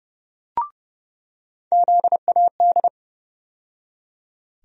Morse code, heard as a keyed tone, about 30 wpm, 700 Hz. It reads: ZAD
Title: USA Call Sign Postfixes - 3 Letters 30wpm